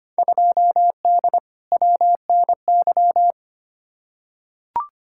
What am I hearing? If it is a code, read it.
Morse code, heard as a keyed tone, about 25 wpm, 700 Hz. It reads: 2D WNY
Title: Field Day Exchanges - 25wpm